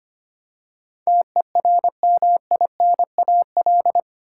Morse code, heard as a keyed tone, about 25 wpm, 700 Hz. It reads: TERMINAL
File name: 8 Letter Words - First 3 Letters Given - 25wpm